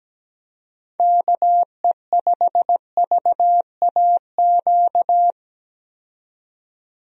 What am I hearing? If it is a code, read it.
Morse code, heard as a keyed tone, about 17 wpm, 700 Hz. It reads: KE5VAQ
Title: USA Call Signs - 6 Characters - 17wpm